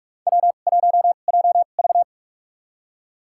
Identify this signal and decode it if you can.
Morse code, heard as a keyed tone, about 45 wpm, spaced as for 32 wpm, 700 Hz. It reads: W1JV
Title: USA Call Signs - 4 Characters - 2x Character Spacing - 45wpm